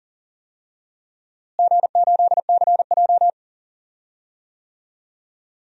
Morse code, heard as a keyed tone, about 40 wpm, 700 Hz. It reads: G8CJ
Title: DXCC-100 Call Signs - 4 Characters - Repeated 3x - 40wpm